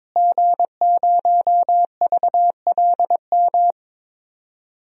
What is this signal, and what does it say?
Morse code, heard as a keyed tone, about 22 wpm, 700 Hz. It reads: G0VLM
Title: DXCC-100 Call Signs - 5 Characters - Repeated 2x - 22wpm